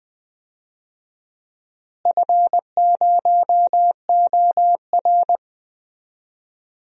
Morse code, heard as a keyed tone, about 20 wpm, 700 Hz. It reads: F0OR
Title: DXCC-100 Call Signs - 4 Characters - Repeated 2x - 20wpm